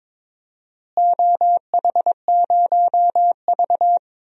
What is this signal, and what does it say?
Morse code, heard as a keyed tone, about 22 wpm, 700 Hz. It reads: OH0V
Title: DXCC-100 Call Signs - 4 Characters - Repeated 2x - 22wpm